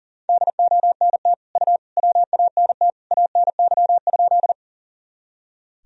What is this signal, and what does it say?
Morse code, heard as a keyed tone, about 40 wpm, 700 Hz. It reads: DONT U WANT ANY?
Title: Sentences from Top 300 Words and Abbreviations 40wpm